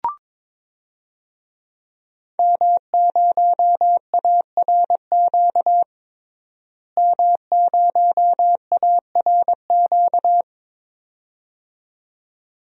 Morse code, heard as a keyed tone, about 22 wpm, 700 Hz. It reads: M0ARQ M0ARQ
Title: DXCC-100 Call Signs - 5 Characters - Repeated 2x - 22wpm